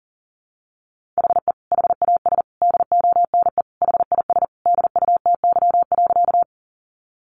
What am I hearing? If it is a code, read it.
Morse code, heard as a keyed tone, about 40 wpm, 700 Hz. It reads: HE HAS DONE HIS DUTY.
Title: Sentences from Top 1500 Words 40wpm